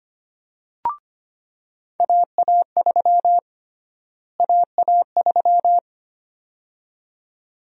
Morse code, heard as a keyed tone, about 25 wpm, 700 Hz. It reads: AA3 AA3
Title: USA Call Sign Prefixes - Repeated 2x - 25wpm